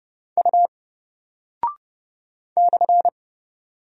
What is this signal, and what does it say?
Morse code, heard as a keyed tone, about 30 wpm, 700 Hz. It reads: U /
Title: Single Character (Letter:Number:Punctuation:Prosign) Alt 30wpm